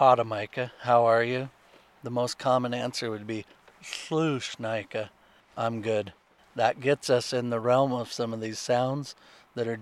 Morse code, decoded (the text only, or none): none